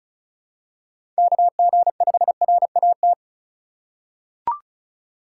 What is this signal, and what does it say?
Morse code, heard as a keyed tone, about 34 wpm, 700 Hz. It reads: KG5RAT
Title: USA Call Signs - 6 Characters - 3x Character Spacing - 35wpm